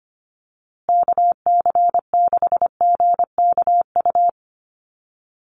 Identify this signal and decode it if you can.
Morse code, heard as a keyed tone, about 25 wpm, 700 Hz. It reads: KC6GKU
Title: USA Call Signs Repeated 3x 25wpm